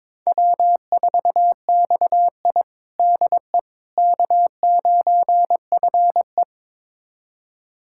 Morse code, heard as a keyed tone, about 22 wpm, 700 Hz. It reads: W4XI DE K9FE
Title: USA Callsign DE Callsign - 4 Characters - 22wpm